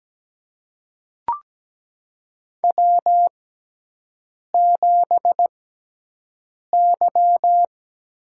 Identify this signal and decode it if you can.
Morse code, heard as a keyed tone, about 17 wpm, spaced as for 6 wpm, 700 Hz. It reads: W7Y
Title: USA Call Signs - 4 Characters - 6x Character Spacing - 17wpm